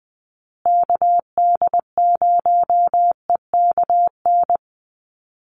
Morse code, heard as a keyed tone, about 20 wpm, 700 Hz. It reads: KD0EKN
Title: USA Call Signs Repeated 3x 20wpm